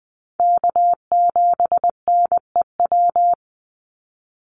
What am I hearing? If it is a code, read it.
Morse code, heard as a keyed tone, about 20 wpm, 700 Hz. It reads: K7NEW
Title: NA SOTA Chaser Callsigns 20wpm